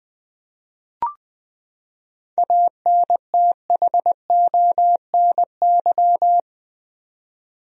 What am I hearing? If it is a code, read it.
Morse code, heard as a keyed tone, about 21 wpm, 700 Hz. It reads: ANTHONY
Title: USA Common Names - 20wpm